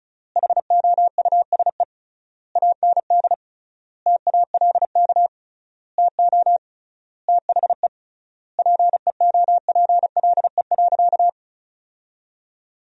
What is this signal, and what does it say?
Morse code, heard as a keyed tone, about 36 wpm, 700 Hz. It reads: HOUSE AND TALK TO THE PEOPLE.
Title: 3x Word-Spacing - Sentences from Top 400 Words - 35wpm